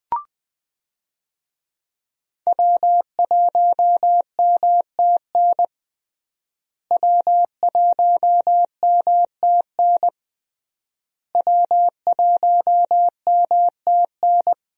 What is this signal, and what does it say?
Morse code, heard as a keyed tone, about 20 wpm, 700 Hz. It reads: W1MTN W1MTN W1MTN
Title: USA Call Signs - 5 Characters - Repeated 3x - 20wpm